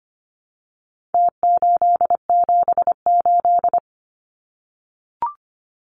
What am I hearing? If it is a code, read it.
Morse code, heard as a keyed tone, about 25 wpm, 700 Hz. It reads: T878
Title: Numbers - Contest Style - 4-Digit 25wpm